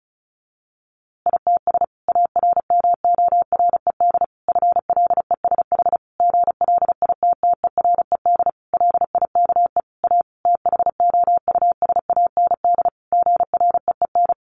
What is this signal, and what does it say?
Morse code, heard as a keyed tone, about 34 wpm, 700 Hz. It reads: ITS ARMORED FLESH GLITTERED LIKE A THOUSAND GREEN
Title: A Sound Of Thunder 35wpm